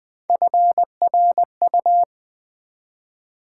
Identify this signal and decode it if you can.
Morse code, heard as a keyed tone, about 20 wpm, 700 Hz. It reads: FRU
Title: Build A Word - 1 Letter At A Time - 3 to 5-Letter Words - 20wpm